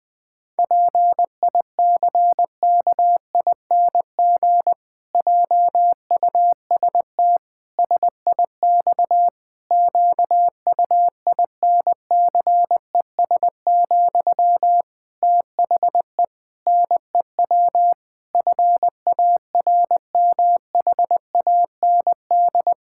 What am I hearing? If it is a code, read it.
Morse code, heard as a keyed tone, about 20 wpm, 700 Hz. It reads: PICKING JUST SIX QUINCES, THE NEW FARMHAND
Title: Pangram Challenge v2 - 20wpm